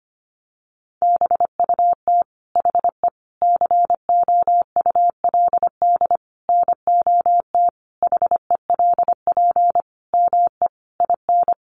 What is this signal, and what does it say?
Morse code, heard as a keyed tone, about 25 wpm, 700 Hz. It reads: BUT HE COULD NOT HELP ME IN
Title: Sentences from Top 600 Words 25wpm